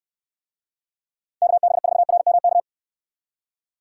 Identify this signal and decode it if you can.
Morse code, heard as a keyed tone, about 63 wpm, 700 Hz. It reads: KB4RMK